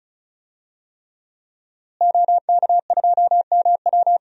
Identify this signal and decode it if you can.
Morse code, heard as a keyed tone, about 36 wpm, 700 Hz. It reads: OK2MW
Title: DXCC-100 Call Signs - 5 Characters - Repeated 3x - 35wpm